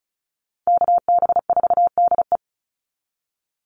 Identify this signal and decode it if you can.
Morse code, heard as a keyed tone, about 34 wpm, 700 Hz. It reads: KB4DE
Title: Call Signs - 35wpm